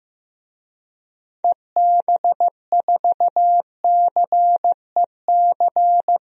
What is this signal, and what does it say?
Morse code, heard as a keyed tone, about 15 wpm, 700 Hz. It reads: EB4CEC
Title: DXCC-100 Call Signs - 6 Characters - Repeated 3x - 15wpm